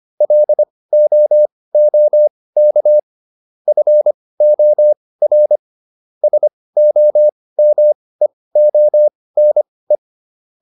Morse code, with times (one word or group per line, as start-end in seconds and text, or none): 0.2-3.0 LOOK
3.7-5.6 FOR
6.2-10.0 SOMEONE